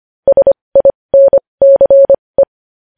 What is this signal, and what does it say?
Morse code, heard as a keyed tone, about 25 wpm, spaced as for 20 wpm, 550 Hz. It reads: SINCE
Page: News headlines in morse code at 20 wpm